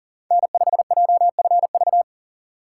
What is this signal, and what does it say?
Morse code, heard as a keyed tone, about 40 wpm, 700 Hz. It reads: N5JFV
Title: USA Call Signs - 5 Characters - 2x Character Spacing - 40wpm